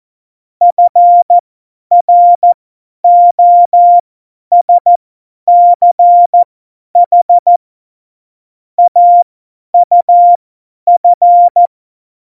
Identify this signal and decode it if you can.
Morse code, heard as a keyed tone, about 14 wpm, spaced as for 10 wpm, 700 Hz. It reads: FROSCH AUF
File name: Morsekurs / Witze / Witz 27 March 2018, 50 F70